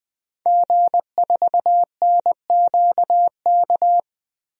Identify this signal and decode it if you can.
Morse code, heard as a keyed tone, about 20 wpm, 700 Hz. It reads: G4NQK
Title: DXCC-100 Call Signs - 5 Characters - 20wpm